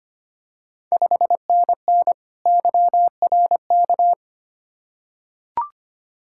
Morse code, heard as a keyed tone, about 25 wpm, 700 Hz. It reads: 5NN YRK
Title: Ontario QSO Party Exchange 25wpm